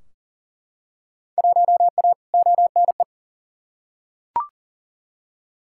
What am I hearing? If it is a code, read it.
Morse code, heard as a keyed tone, about 40 wpm, 700 Hz. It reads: 1A ONE
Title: Field Day Exchanges - 40wpm